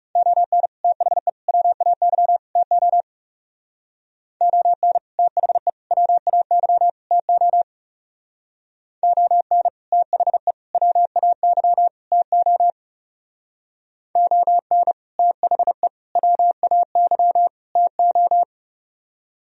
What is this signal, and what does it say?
Morse code, heard as a keyed tone, about 36 wpm, 700 Hz. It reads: ON THE WAY TO ON THE WAY TO ON THE WAY TO ON THE WAY TO
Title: Speed-Racing - Sets of 4 Words 30wpm